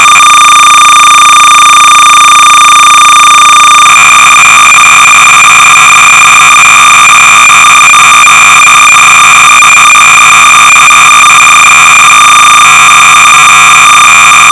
Cassette data (Kansas City type).
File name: sorcerer / 300baud / Memory Change-(easy Hex entry)-(P.Grimshaw)-Sorcerer-300baud